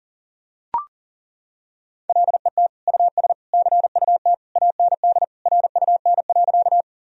Morse code, heard as a keyed tone, about 40 wpm, 700 Hz. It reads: LET US CUT AND RUN.
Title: Sentences from Top 1000 Words - Review Part 1 - 40wpm